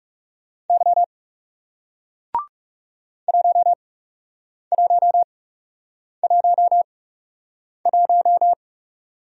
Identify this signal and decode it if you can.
Morse code, heard as a keyed tone, about 40 wpm, spaced as for 8 wpm, 700 Hz. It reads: Y 1111